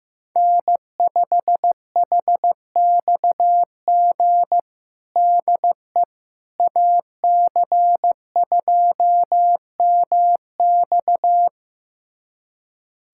Morse code, heard as a keyed tone, about 15 wpm, 700 Hz. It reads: N5HXG DE AC2MX
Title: USA Callsign DE Callsign - 5 Characters - 15wpm